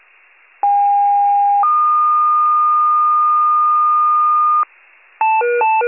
Two-tone page, heard as a one-second tone, then a three-second tone, about 810 then 1220 Hz, with noise. It is heard from 0.6 to 4.6 s.